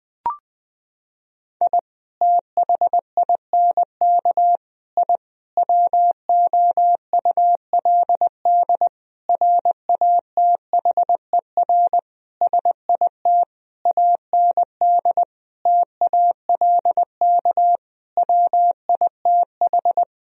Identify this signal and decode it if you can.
Morse code, heard as a keyed tone, about 21 wpm, 700 Hz. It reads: I THINK I WOULD RATHER SIT AND TALK WITH
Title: Sentences from Top 1000 Words - Review Part 4 - 20wpm